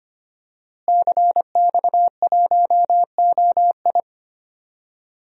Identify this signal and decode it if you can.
Morse code, heard as a keyed tone, about 25 wpm, 700 Hz. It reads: CX1OI
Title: DXCC-100 Call Signs - 5 Characters - Repeated 2x - 25wpm